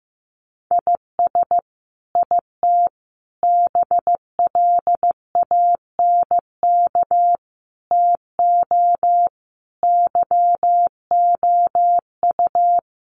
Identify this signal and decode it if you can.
Morse code, heard as a keyed tone, about 15 wpm, 700 Hz. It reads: IS IT BLANK TO YOU
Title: Sentences from Top 1500 Words 15wpm